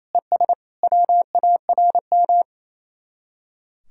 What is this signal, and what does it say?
Morse code, heard as a keyed tone, about 33 wpm, 700 Hz. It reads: ES WARM
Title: Speed-Racing Common QSO Phrases Encore - 28wpm